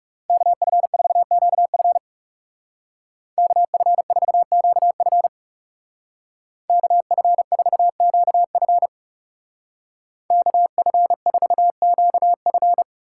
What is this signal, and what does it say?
Morse code, heard as a keyed tone, about 37 wpm, 700 Hz. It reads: KF4QF KF4QF KF4QF KF4QF
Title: Speed Racing - USA 2x2 Call Signs 30wpm